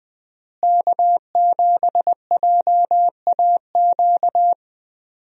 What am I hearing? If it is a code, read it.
Morse code, heard as a keyed tone, about 20 wpm, 700 Hz. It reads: K7JAQ
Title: USA Call Signs - 5 Characters - Repeated 2x - 20wpm